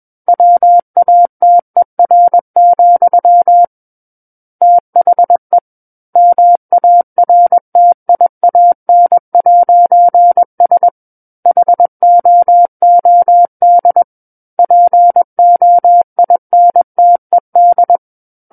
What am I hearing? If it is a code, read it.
Morse code, heard as a keyed tone, about 21 wpm, 700 Hz. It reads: WATER, THE MARTIAN'S HOOD POINTED
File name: Book One / 12-Chapter 12 - 21 WPM